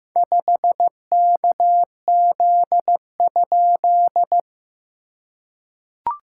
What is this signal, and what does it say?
Morse code, heard as a keyed tone, about 15 wpm, 700 Hz. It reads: 5KZ?
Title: SOTA QSO Practice 15wpm